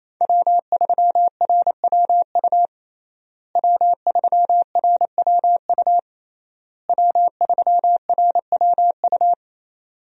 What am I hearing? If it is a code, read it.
Morse code, heard as a keyed tone, about 28 wpm, 700 Hz. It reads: W3RWU W3RWU W3RWU
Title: USA Call Signs - 5 Characters - Repeated 3x - 28wpm